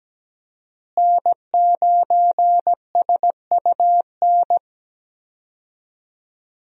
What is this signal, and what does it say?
Morse code, heard as a keyed tone, about 17 wpm, 700 Hz. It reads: N9SUN
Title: USA Call Signs - 5 Characters - Repeated 3x - 17wpm